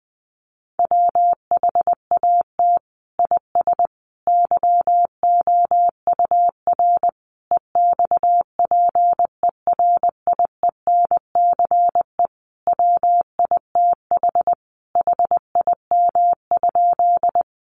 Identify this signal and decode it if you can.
Morse code, heard as a keyed tone, about 20 wpm, 700 Hz. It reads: WHAT IS YOUR EXPERIENCE WITH HIM?
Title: Sentences from Top 600 Words 20wpm